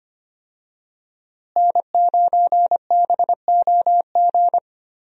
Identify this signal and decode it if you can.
Morse code, heard as a keyed tone, about 25 wpm, 700 Hz. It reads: N9BOG
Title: USA Call Signs - 5 Characters - Repeated 3x - 25wpm